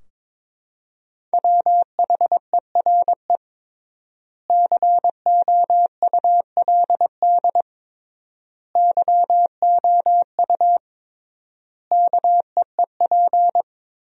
Morse code, heard as a keyed tone, about 22 wpm, 700 Hz. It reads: WHERE COULD YOU KEEP
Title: 3x Word-Spacing - Sentences from Top 300 Words - 22wpm